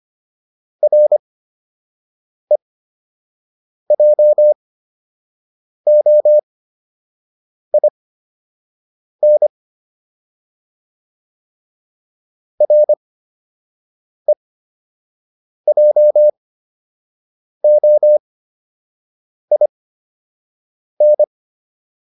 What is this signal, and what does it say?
Morse code, heard as a keyed tone, about 25 wpm, spaced as for 6 wpm, 600 Hz. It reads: REJOIN REJOIN